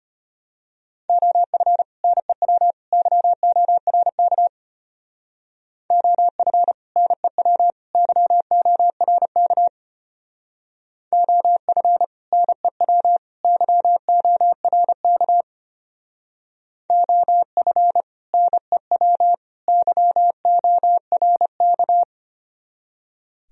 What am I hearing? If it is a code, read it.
Morse code, heard as a keyed tone, about 32 wpm, 700 Hz. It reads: OF NEW YORK OF NEW YORK OF NEW YORK OF NEW YORK